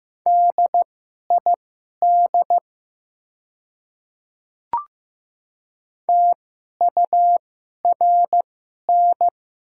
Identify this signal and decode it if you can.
Morse code, heard as a keyed tone, about 15 wpm, spaced as for 11 wpm, 700 Hz. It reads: DID TURN
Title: 2x Character Spacing - Top 200 Words - 15wpm